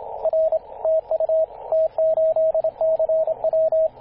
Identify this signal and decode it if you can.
Morse code, heard as a keyed tone, about 27 wpm, 640 Hz, with radio noise. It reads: R TU T8CW
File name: t8cw-18-0737mhz-cw